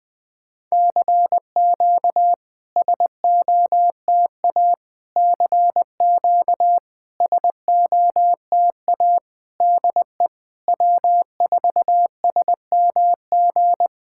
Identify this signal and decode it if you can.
Morse code, heard as a keyed tone, about 20 wpm, 700 Hz. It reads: CQ SOTA CQ SOTA DE W4SMG
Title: SOTA QSO Practice 20wpm